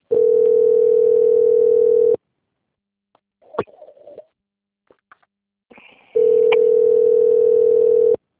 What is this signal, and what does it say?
Ringback tone (North America), 440 + 480 Hz.
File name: out- - - - - .34541